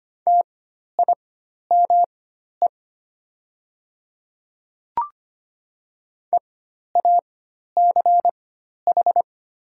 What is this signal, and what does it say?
Morse code, heard as a keyed tone, about 25 wpm, spaced as for 12 wpm, 700 Hz. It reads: TIME EACH
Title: 4x Character Spacing - Top 100 Words - 25wpm